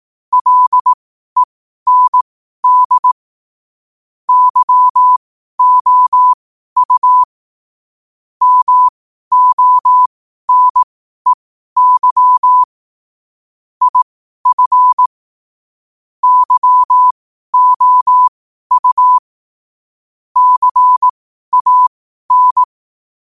Morse code, heard as a keyed tone, about 18 wpm, spaced as for 12 wpm, 1000 Hz. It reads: LEND YOU MONEY IF YOU CAN
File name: qotdfiles / 12wpm